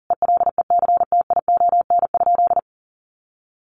Morse code, heard as a keyed tone, about 40 wpm, 700 Hz. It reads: ELECTION?
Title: Sentences from Top 1500 Words 40wpm